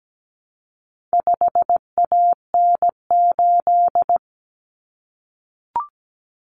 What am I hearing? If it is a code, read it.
Morse code, heard as a keyed tone, about 17 wpm, 700 Hz. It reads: 5AN8
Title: Common Cut Numbers - 4-Digit 17wpm